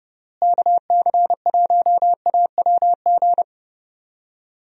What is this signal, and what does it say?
Morse code, heard as a keyed tone, about 29 wpm, 700 Hz. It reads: KC1AWG
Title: USA Call Signs - 6 Characters - Repeated 3x - 30wpm